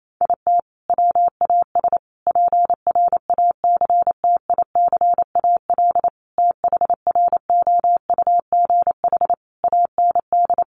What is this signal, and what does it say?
Morse code, heard as a keyed tone, about 28 wpm, 700 Hz. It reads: IT WAS PRACTICAL THROUGH AND
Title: Sentences from Top 1400 Words 28wpm